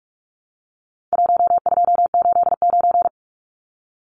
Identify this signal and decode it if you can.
Morse code, heard as a keyed tone, about 44 wpm, 700 Hz. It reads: 1289
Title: Numbers - Contest Style - 4-Digit 45wpm